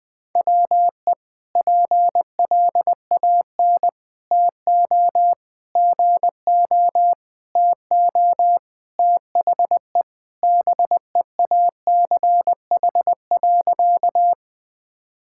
Morse code, heard as a keyed tone, about 21 wpm, 700 Hz. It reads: WE PLAN TO GO TO THE BEACH.
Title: Sentences with 4-Word Phrases - Spoken Beginning - 20wpm